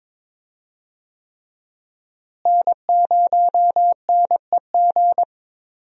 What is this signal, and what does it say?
Morse code, heard as a keyed tone, about 22 wpm, 700 Hz. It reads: N0NEG
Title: USA Call Signs - 5 Characters - Repeated 3x - 22wpm